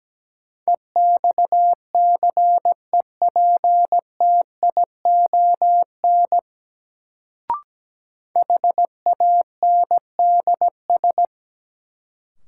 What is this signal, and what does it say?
Morse code, heard as a keyed tone, about 17 wpm, 700 Hz. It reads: EXCEPTION HANDS
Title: Sets of 4 Words - New Word Drill - 17wpm